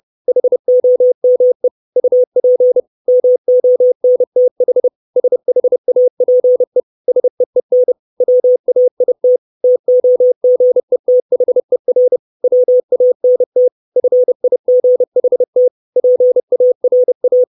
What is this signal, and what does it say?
Morse code, heard as a keyed tone, about 30 wpm, 500 Hz. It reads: HOME UP MONTH SHAPE SEEN WAIT TOGETHER WANT FIGHT PARA
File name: 632 Words 30WPM